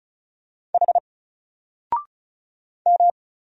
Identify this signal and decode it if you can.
Morse code, heard as a keyed tone, about 36 wpm, 700 Hz. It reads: H M
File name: Single-Letter-Number-35wpm